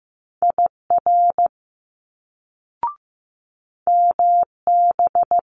Morse code, heard as a keyed tone, about 15 wpm, 700 Hz. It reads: IR MB
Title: Multi-letter Phonemes Quick Fire 15wpm